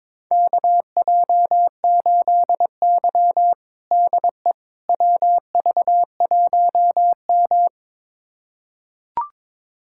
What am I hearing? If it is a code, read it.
Morse code, heard as a keyed tone, about 22 wpm, 700 Hz. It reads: KJ8Y DE WV1M
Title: USA Callsign DE Callsign - 4 Characters - 22wpm